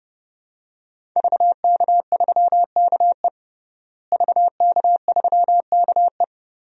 Morse code, heard as a keyed tone, about 30 wpm, 700 Hz. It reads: VK3KE VK3KE
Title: DXCC-100 Call Signs - 5 Characters - Repeated 3x - 30wpm